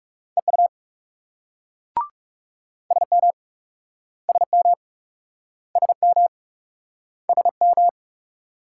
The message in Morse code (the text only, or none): EU SM SM SM SM